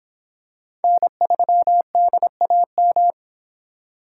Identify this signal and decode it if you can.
Morse code, heard as a keyed tone, about 26 wpm, 700 Hz. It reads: N3DAM